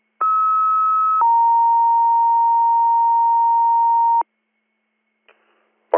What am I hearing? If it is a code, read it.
Two-tone page, heard as a one-second tone, then a three-second tone, about 1290 then 930 Hz. It is heard from 0.2 to 4.2 s.